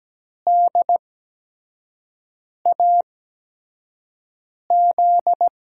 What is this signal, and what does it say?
Morse code, heard as a keyed tone, about 17 wpm, spaced as for 5 wpm, 700 Hz. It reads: DAZ
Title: USA Call Signs - 5 Characters - 8x Character Spacing - 17wpm